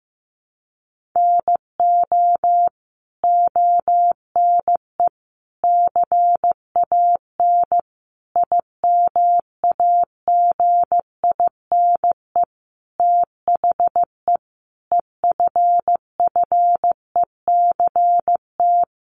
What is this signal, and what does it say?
Morse code, heard as a keyed tone, about 15 wpm, 700 Hz. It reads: NO ONE CAN IMAGINE THE EFFECT
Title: Sentences from Top 1400 Words 15wpm